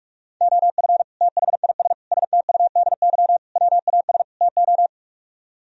Morse code, heard as a keyed tone, about 50 wpm, 700 Hz. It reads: OF THIS STUDY WAS TO